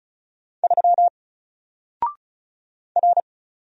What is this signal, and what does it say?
Morse code, heard as a keyed tone, about 33 wpm, 700 Hz. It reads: 3 R
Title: Single Character (Letter:Number:Punctuation:Prosign) 35wpm